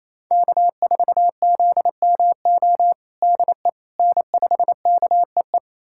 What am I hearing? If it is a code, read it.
Morse code, heard as a keyed tone, about 28 wpm, 700 Hz. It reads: K4ZMO DE N5KEE
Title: USA Callsign DE Callsign - 5 Characters - 28wpm